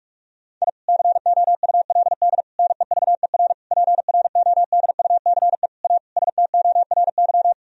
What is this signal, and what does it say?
Morse code, heard as a keyed tone, about 46 wpm, 700 Hz. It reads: I COULD NEVER PRODUCE A STORY
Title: Sentences from Top 1000 Words - Review Part 3 - 45wpm